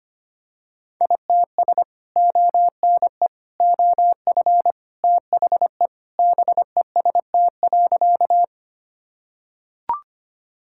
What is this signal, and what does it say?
Morse code, heard as a keyed tone, about 25 wpm, 700 Hz. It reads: ITS ONE OF THE BEST.
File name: practice / 25wpm / Sentences with 3-Word Phrases - 25wpm